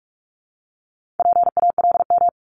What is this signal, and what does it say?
Morse code, heard as a keyed tone, about 46 wpm, 700 Hz. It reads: PALM
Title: Multi-letter Phonemes Repeated 3x 45wpm